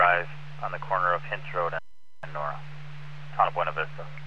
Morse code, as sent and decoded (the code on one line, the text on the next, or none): none